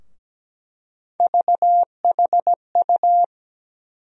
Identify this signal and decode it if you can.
Morse code, heard as a keyed tone, about 17 wpm, 700 Hz. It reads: VHU
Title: USA Call Sign Postfixes - 3 Letters 17wpm